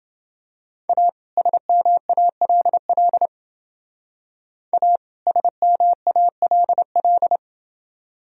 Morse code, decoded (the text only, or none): A SMALL A SMALL